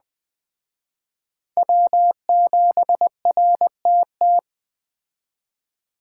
Morse code, heard as a keyed tone, about 20 wpm, 700 Hz. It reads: W7RTT
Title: Speed Racing - USA 1x3 Call Signs 17wpm